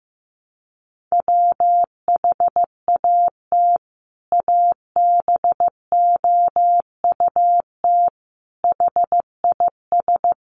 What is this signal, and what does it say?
Morse code, heard as a keyed tone, about 14 wpm, 700 Hz. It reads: WHAT ABOUT HIS
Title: Sentences from Top 1100 Words 15wpm